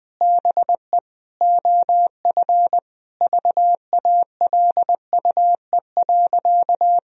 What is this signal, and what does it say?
Morse code, heard as a keyed tone, about 20 wpm, 700 Hz. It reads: BE OF VALUE.
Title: Sentences from Top 1000 Words - Review Part 3 - 20wpm